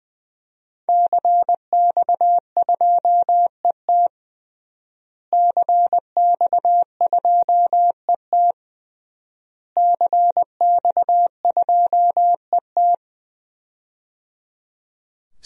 Morse code, heard as a keyed tone, about 20 wpm, 700 Hz. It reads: CX2ET CX2ET CX2ET
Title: DXCC-100 Call Signs - 5 Characters - Repeated 3x - 20wpm